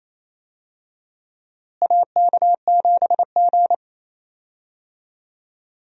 Morse code, heard as a keyed tone, about 28 wpm, 700 Hz. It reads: AK7G
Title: USA Call Signs - 4 Characters - Repeated 2x - 28wpm